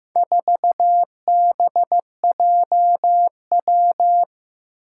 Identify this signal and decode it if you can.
Morse code, heard as a keyed tone, about 15 wpm, 700 Hz. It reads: 4BJW
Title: Four Challenging Letter-Number Combinations 15wpm